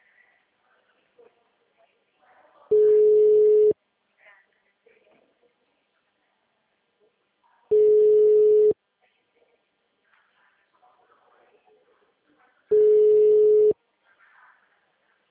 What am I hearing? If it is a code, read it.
Ringback tone (Europe, 425 Hz), with speech.